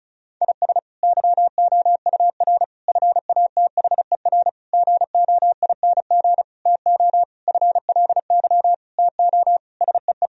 Morse code, heard as a keyed tone, about 36 wpm, 700 Hz. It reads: IS YOUR FATHER GOING TO FLY TO SEE
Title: Sentences from Top 1000 Words - Review Part 2 - 35wpm